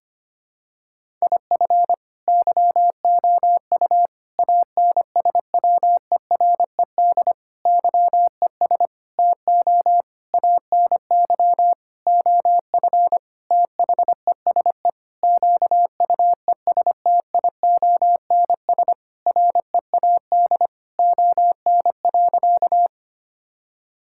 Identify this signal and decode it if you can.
Morse code, heard as a keyed tone, about 25 wpm, 700 Hz. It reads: IF YOU ANSWERED YES TO ANY OF THESE QUESTIONS READ ON.
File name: practice / 25wpm / Sentences with 9-Word Phrases - 25wpm